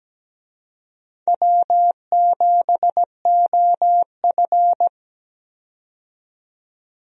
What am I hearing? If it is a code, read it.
Morse code, heard as a keyed tone, about 17 wpm, 700 Hz. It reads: W7OF
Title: USA Call Signs - 4 Characters - 17wpm